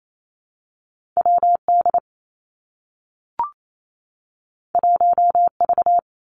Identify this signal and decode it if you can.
Morse code, heard as a keyed tone, about 28 wpm, 700 Hz. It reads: W D 1 V
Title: Two Challenging Letter-Number Combinations 28wpm